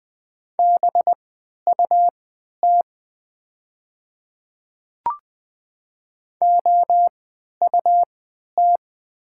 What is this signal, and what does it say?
Morse code, heard as a keyed tone, about 20 wpm, spaced as for 11 wpm, 700 Hz. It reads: BUT OUT
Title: 3x Character Spacing - Top 500 Words - 20wpm